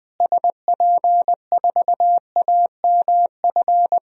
Morse code, heard as a keyed tone, about 20 wpm, 700 Hz. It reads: SP4AMF
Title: DXCC-100 Call Signs - 6 Characters - Repeated 2x - 20wpm